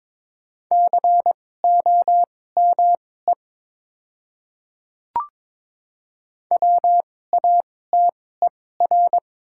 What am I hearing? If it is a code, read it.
Morse code, heard as a keyed tone, about 22 wpm, spaced as for 16 wpm, 700 Hz. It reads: COME WATER
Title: 2x Character Spacing - Top 100 Words - 22wpm